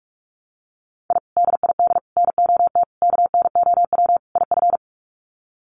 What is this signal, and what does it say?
Morse code, heard as a keyed tone, about 44 wpm, 700 Hz. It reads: I DID NOT KNOW IF
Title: Sets of 5 Words Encore 45wpm